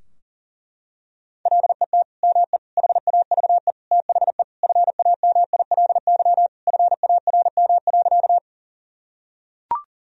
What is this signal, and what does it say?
Morse code, heard as a keyed tone, about 40 wpm, 700 Hz. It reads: LET ME HAVE THE FAMILY FARM.
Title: Sentences from Top 1000 Words - Review Part 2 - 40wpm